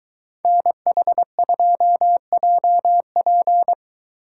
Morse code, heard as a keyed tone, about 22 wpm, 700 Hz. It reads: NH2JP